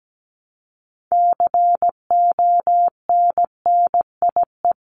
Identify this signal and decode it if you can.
Morse code, heard as a keyed tone, about 17 wpm, 700 Hz. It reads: CONNIE
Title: CWT Exchange 17wpm